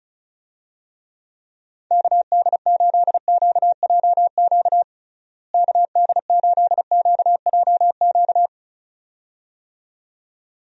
Morse code, heard as a keyed tone, about 34 wpm, 700 Hz. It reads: KD8QJQ KD8QJQ
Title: USA Call Signs - 6 Characters - Repeated 2x - 35wpm